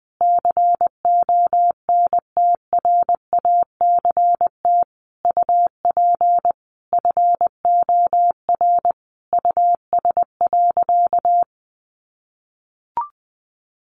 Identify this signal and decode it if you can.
Morse code, heard as a keyed tone, about 20 wpm, 700 Hz. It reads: CONTRACT UP FOR US.
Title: Sentences from Top 1400 Words 20wpm